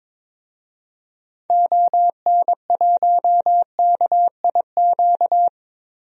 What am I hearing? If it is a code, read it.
Morse code, heard as a keyed tone, about 21 wpm, 700 Hz. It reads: ON1KIQ
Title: DXCC-100 Call Signs - 6 Characters - Repeated 2x - 22wpm